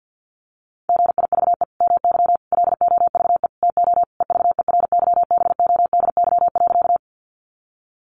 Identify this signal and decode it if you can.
Morse code, heard as a keyed tone, about 50 wpm, 700 Hz. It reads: GIVE MY LOVE TO EVERYBODY.
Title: Sentences from Top 1400 Words 50wpm